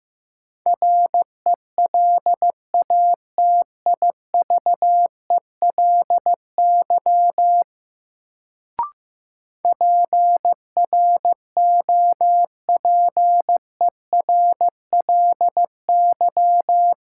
Morse code, heard as a keyed tone, about 15 wpm, 700 Hz. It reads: RELATIVELY PROPERLY
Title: Sentences from Top 1100 Words - New Word Drill Alt - 15wpm